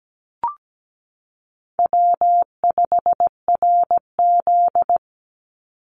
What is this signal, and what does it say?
Morse code, heard as a keyed tone, about 17 wpm, 700 Hz. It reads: W5RZ
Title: Easy Call Signs - 17wpm